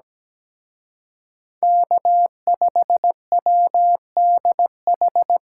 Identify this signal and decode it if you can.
Morse code, heard as a keyed tone, about 20 wpm, 700 Hz. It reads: K5WDH
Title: Speed Racing - USA 1x3 Call Signs 17wpm